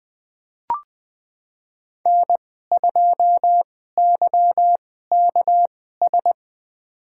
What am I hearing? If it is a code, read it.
Morse code, heard as a keyed tone, about 20 wpm, spaced as for 14 wpm, 700 Hz. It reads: N2YKS
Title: USA Call Signs - 5 Characters - 2x Character Spacing - 20wpm